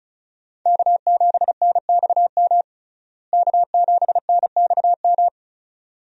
Morse code, heard as a keyed tone, about 34 wpm, 700 Hz. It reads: K7NXM K7NXM
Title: USA Call Signs - 5 Characters - Repeated 2x - 35wpm